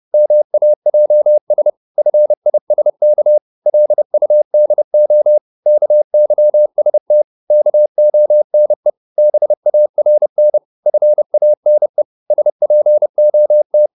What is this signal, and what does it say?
Morse code, heard as a keyed tone, about 30 wpm, 600 Hz. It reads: MAJS FISK LUDO KYST KONE BARN FANE SPOT
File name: Korte ord DK 30wpm